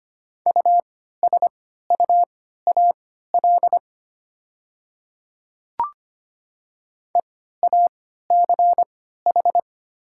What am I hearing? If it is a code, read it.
Morse code, heard as a keyed tone, about 25 wpm, spaced as for 14 wpm, 700 Hz. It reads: USUAL EACH
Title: 3x Character Spacing - Top 500 Words - 25wpm